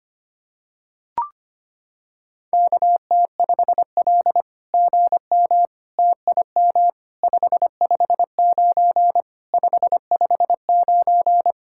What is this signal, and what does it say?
Morse code, heard as a keyed tone, about 25 wpm, 700 Hz. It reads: KT5L GM TIM 559 559
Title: SOTA QSO Practice 25wpm